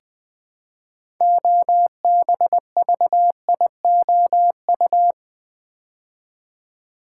Morse code, heard as a keyed tone, about 20 wpm, 700 Hz. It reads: OBVIOU
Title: Build A Word - 3 Letters At A Time - 9-Letter Words - 20wpm